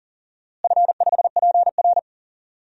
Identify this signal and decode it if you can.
Morse code, heard as a keyed tone, about 40 wpm, 700 Hz. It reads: F5PR
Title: DXCC-100 Call Signs - 4 Characters - Repeated 3x - 40wpm